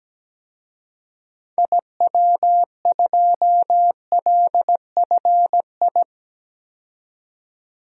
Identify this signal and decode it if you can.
Morse code, heard as a keyed tone, about 17 wpm, 700 Hz. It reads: IW2LFI
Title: DXCC-100 Call Signs - 6 Characters - 17wpm